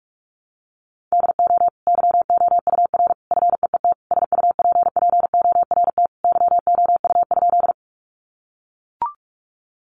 Morse code, heard as a keyed tone, about 44 wpm, 700 Hz. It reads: DO YOUR FEET SUPPORT YOU?
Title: Sentences from Top 800 Words 45wpm